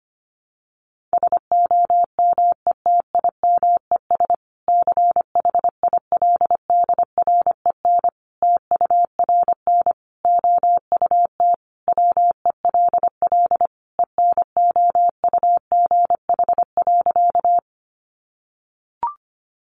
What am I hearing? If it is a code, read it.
Morse code, heard as a keyed tone, about 25 wpm, 700 Hz. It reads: SOMETIMES CHILDREN TURN OUT WELL ENOUGH.
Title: Sentences from Top 600 Words 25wpm